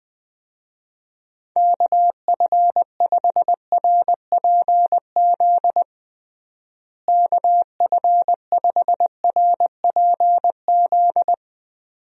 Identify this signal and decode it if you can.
Morse code, heard as a keyed tone, about 21 wpm, 700 Hz. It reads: KF5RPZ KF5RPZ
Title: USA Call Signs - 6 Characters - Repeated 2x - 20wpm